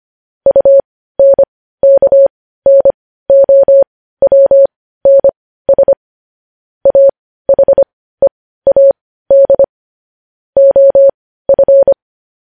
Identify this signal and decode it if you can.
Morse code, heard as a keyed tone, about 25 wpm, spaced as for 15 wpm, 550 Hz. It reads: UNKNOWNS AHEAD OF
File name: podcast / News headlines-1x-25x15